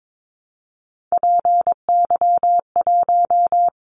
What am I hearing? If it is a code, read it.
Morse code, heard as a keyed tone, about 22 wpm, 700 Hz. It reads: PY1
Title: Callsign Prefixes for Top 100 DXCC - 22wpm